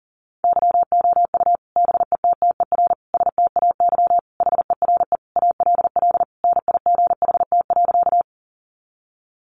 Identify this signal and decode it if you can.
Morse code, heard as a keyed tone, about 40 wpm, 700 Hz. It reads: YOU BETTER STAY HERE ALL NIGHT.
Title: Sentences from Top 500 Words 40wpm